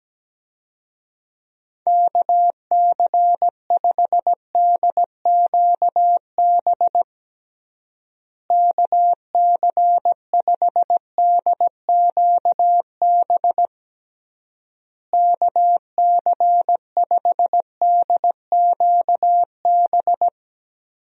Morse code, heard as a keyed tone, about 17 wpm, 700 Hz. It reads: KC5DQB KC5DQB KC5DQB
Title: USA Call Signs - 6 Characters - Repeated 3x - 17wpm